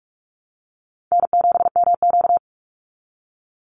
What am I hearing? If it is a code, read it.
Morse code, heard as a keyed tone, about 44 wpm, 700 Hz. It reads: N7MQ
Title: Easy Call Signs Repeated 3x - 45wpm